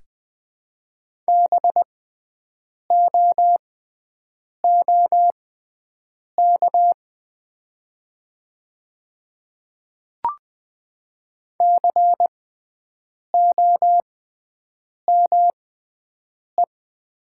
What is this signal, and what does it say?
Morse code, heard as a keyed tone, about 20 wpm, spaced as for 7 wpm, 700 Hz. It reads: BOOK COME
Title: 6x Character Spacing - Top 500 Words - 20wpm